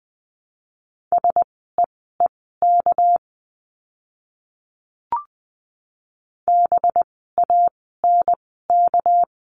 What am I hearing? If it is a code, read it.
Morse code, heard as a keyed tone, about 20 wpm, spaced as for 14 wpm, 700 Hz. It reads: SEEK BANK
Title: 199 Lesson 32f - Letter k - Words 20wpm